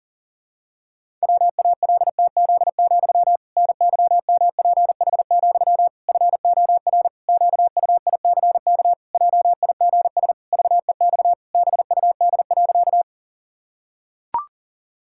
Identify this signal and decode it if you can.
Morse code, heard as a keyed tone, about 37 wpm, 700 Hz. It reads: WALTZ, NYMPH, FOR QUICK JIGS VEX BUD.
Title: Pangram Challenge v2 - 40wpm